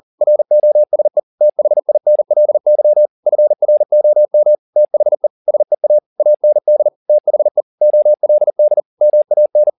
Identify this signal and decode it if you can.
Morse code, heard as a keyed tone, about 40 wpm, 600 Hz. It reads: ROSE THINLY FROM THE SEA AND THE OLD MAN